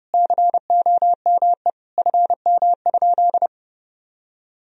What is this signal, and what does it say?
Morse code, heard as a keyed tone, about 30 wpm, 700 Hz. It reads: COME FM?
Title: Sentences from Top 200 Words and Abbreviations 30wpm